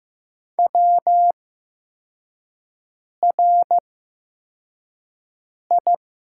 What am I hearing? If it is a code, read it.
Morse code, heard as a keyed tone, about 15 wpm, spaced as for 4 wpm, 700 Hz. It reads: WRI
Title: 8x Character Spacing - Top 300 Words - 15wpm